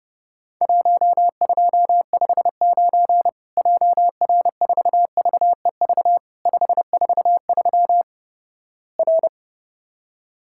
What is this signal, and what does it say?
Morse code, heard as a keyed tone, about 30 wpm, 700 Hz. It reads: 1259 JR4VEV 543 R
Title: Worked All Europe DX Contest - QTC Exchange - 30wpm